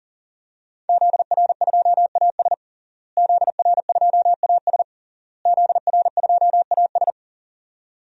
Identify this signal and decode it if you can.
Morse code, heard as a keyed tone, about 40 wpm, 700 Hz. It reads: ZR2AS ZR2AS ZR2AS
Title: DXCC-100 Call Signs - 5 Characters - Repeated 3x - 40wpm